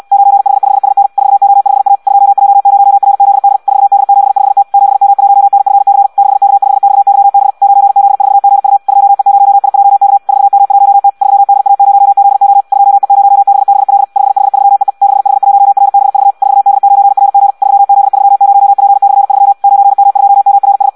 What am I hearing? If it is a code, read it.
Morse code, heard as a keyed tone, about 74 wpm, 800 Hz. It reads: 9H5SN 4O5A 2J0RZD 4D75T 7M1NFC 6K5XYB 9Z4GU 2E0IJK 4M1T 3DA0ZO 2E0BBL 5H3EE 6H1RBV 4D9RG 3Z30P/3 9M2MRS